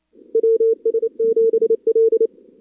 Morse code, heard as a keyed tone, about 27 wpm, 440 Hz. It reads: WS7L
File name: CWSprint / Feb2022 / k5tr-feb-na-sprint-cw-k5ot -0100